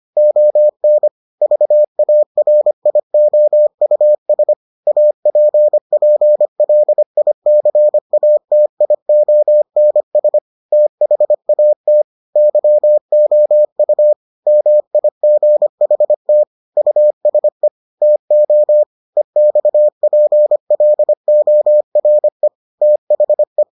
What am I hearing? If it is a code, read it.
Morse code, heard as a keyed tone, about 25 wpm, 600 Hz. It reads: ON VARIOUS APPLICATIONS THAT YOU MIGHT USE TO EXPLORE THE